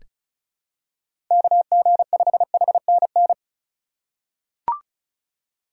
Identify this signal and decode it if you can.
Morse code, heard as a keyed tone, about 36 wpm, 700 Hz. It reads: KG5HNN
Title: USA Call Signs - 4x Character Spacing - 35wpm